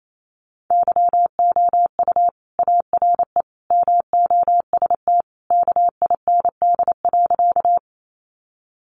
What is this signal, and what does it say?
Morse code, heard as a keyed tone, about 28 wpm, 700 Hz. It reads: YOU ARE MOST KIND.
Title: Sentences from Top 200 Words 28wpm